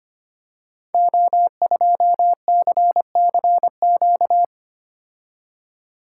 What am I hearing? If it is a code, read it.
Morse code, heard as a keyed tone, about 25 wpm, 700 Hz. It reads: O2CCQ
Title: Code Groups - 5 Letter:Numbers - 25wpm